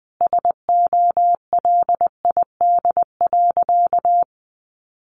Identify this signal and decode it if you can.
Morse code, heard as a keyed tone, about 20 wpm, 700 Hz. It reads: SOLID.
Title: Sentences from Top 1500 Words 20wpm